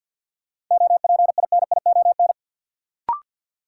Morse code, heard as a keyed tone, about 52 wpm, 700 Hz. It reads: OPINION
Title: Sentences from Top 1200 Words - New Word Drill Alt - 50wpm